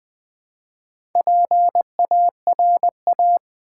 Morse code, heard as a keyed tone, about 20 wpm, 700 Hz. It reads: PARA
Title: Build A Word - 2 Letters At A Time - 8-Letter Words - 20wpm